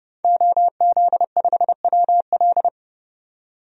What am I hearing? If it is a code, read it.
Morse code, heard as a keyed tone, about 30 wpm, 700 Hz. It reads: OZ5WL
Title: DXCC-100 Call Signs - 5 Characters - 30wpm